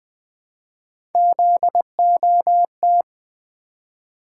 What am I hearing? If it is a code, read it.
Morse code, heard as a keyed tone, about 20 wpm, 700 Hz. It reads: ZOT
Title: Three Letter-Number Combinations - 20wpm